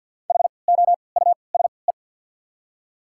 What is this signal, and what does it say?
Morse code, heard as a keyed tone, about 53 wpm, spaced as for 29 wpm, 690 Hz. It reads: HOUSE
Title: 3x Character Spacing - Top 300 Words - 50wpm